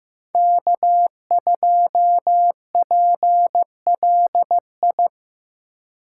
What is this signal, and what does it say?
Morse code, heard as a keyed tone, about 15 wpm, 700 Hz. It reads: K2PLI
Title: USA Call Signs - 3x Character Spacing - 15wpm